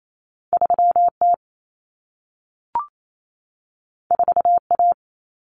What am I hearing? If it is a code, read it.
Morse code, heard as a keyed tone, about 28 wpm, 700 Hz. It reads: 3T 4A
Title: Common Cut Numbers - 2-Digit 28wpm